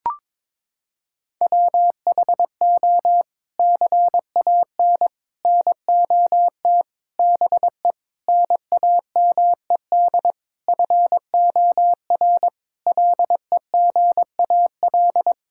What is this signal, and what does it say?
Morse code, heard as a keyed tone, about 22 wpm, 700 Hz. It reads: WHO CAN NOT BE NAMED FOR LEGAL
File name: practice / 22wpm / Sets of 8 Words - 22wpm